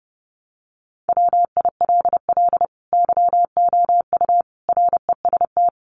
Morse code, heard as a keyed tone, about 30 wpm, 700 Hz. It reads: WILL YOU REST
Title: Sentences from Top 500 Words 30wpm